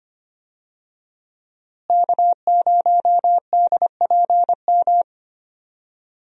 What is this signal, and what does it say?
Morse code, heard as a keyed tone, about 25 wpm, 700 Hz. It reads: K0DPM
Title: USA Call Signs - 5 Characters - Repeated 2x - 25wpm